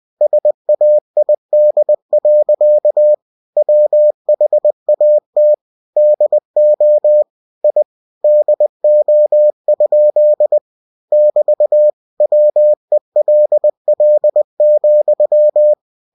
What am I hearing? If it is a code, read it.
Morse code, heard as a keyed tone, about 20 wpm, 600 Hz. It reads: SAID. WHAT DO I DO? <BT> WELL,
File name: cw-books / winnie-the-pooh / winnie-20wpm-page-0065